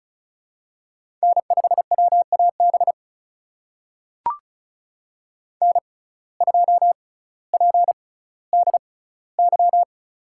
Morse code, heard as a keyed tone, about 35 wpm, spaced as for 18 wpm, 700 Hz. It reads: N5WAB N2PDY
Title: USA Call Signs - 5 Characters - 6x Character Spacing - 35wpm